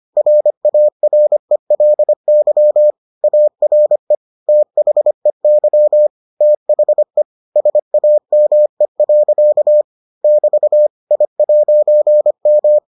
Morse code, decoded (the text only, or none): RARELY ARE THEY THE SAME. <BT> I'M